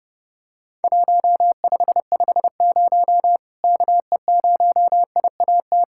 Morse code, heard as a keyed tone, about 30 wpm, 700 Hz. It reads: 1550 KE0IAT
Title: Worked All Europe DX Contest - QTC Exchange - 30wpm